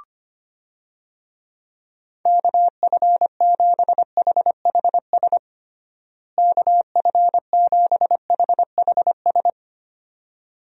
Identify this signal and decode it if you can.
Morse code, heard as a keyed tone, about 25 wpm, 700 Hz. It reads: KF7HHS KF7HHS
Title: USA Call Signs - 6 Characters - Repeated 2x - 25wpm